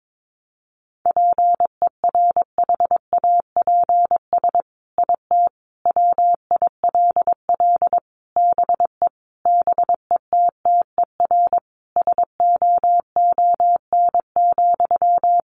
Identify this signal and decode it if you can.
Morse code, heard as a keyed tone, about 22 wpm, 700 Hz. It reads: PERHAPS IT WILL BE BETTER SOON,
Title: Sentences from Top 1500 Words 22wpm